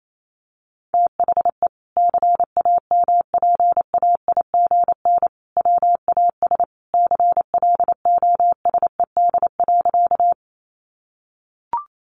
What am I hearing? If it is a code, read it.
Morse code, heard as a keyed tone, about 28 wpm, 700 Hz. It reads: THE CAMPAIGN WAS CLOSED.
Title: Sentences from Top 1400 Words 28wpm